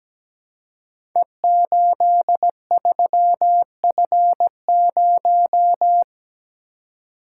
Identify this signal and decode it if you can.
Morse code, heard as a keyed tone, about 17 wpm, 700 Hz. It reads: E83F0
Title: Code Groups - 5 Letter:Numbers - 17wpm